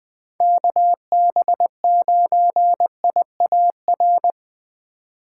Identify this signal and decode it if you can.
Morse code, heard as a keyed tone, about 22 wpm, 700 Hz. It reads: KB9IAR
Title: Speed Racing - USA Call Signs 20wpm